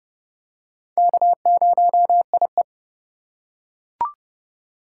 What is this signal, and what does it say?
Morse code, heard as a keyed tone, about 30 wpm, 700 Hz. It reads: K0IE
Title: USA Call Signs - 4 Characters - 4x Character Spacing - 30wpm